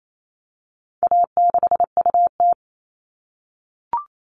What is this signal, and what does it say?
Morse code, heard as a keyed tone, about 28 wpm, 700 Hz. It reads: A6UT
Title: All Cut Numbers - 4-Digit 28wpm